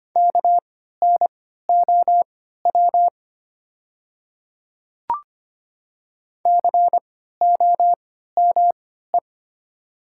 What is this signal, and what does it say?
Morse code, heard as a keyed tone, about 25 wpm, spaced as for 14 wpm, 700 Hz. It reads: KNOW COME
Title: 3x Character Spacing - Top 100 Words - 25wpm